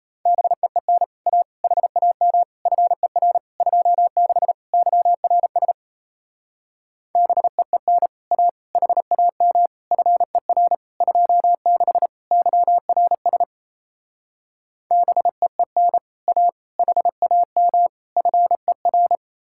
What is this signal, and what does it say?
Morse code, heard as a keyed tone, about 36 wpm, 700 Hz. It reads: BEEN A HAM FER 26 YRS BEEN A HAM FER 26 YRS BEEN A HAM FER